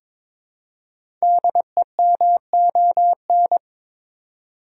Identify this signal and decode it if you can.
Morse code, heard as a keyed tone, about 22 wpm, 700 Hz. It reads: DEMON
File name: 5 Letter Words - First 3 Letters Given - 22wpm